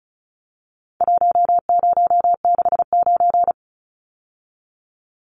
Morse code, heard as a keyed tone, about 34 wpm, 700 Hz. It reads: 1069
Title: Numbers - Contest Style - 4-Digit 35wpm